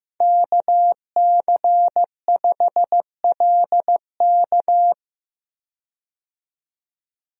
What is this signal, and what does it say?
Morse code, heard as a keyed tone, about 15 wpm, 700 Hz. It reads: KC5LK
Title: USA Call Signs - 5 Characters - 15wpm